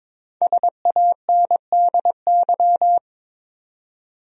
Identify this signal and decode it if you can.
Morse code, heard as a keyed tone, about 22 wpm, 700 Hz. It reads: SANDY
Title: USA Common Names - Max 5 Letters - 22wpm